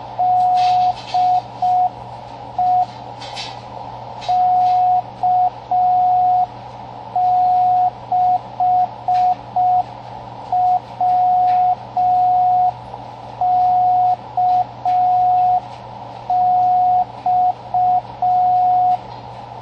Morse code, heard as a keyed tone, about 5 wpm, 730 Hz, with radio noise. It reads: DE K6WKX
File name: CWResources / fun stuff / 07 - Slow Beacon